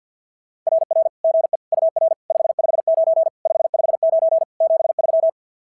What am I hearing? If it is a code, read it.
Morse code, heard as a keyed tone, about 47 wpm, 650 Hz. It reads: RR GE UR 559 559 73
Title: SOTA QSO Practice 50wpm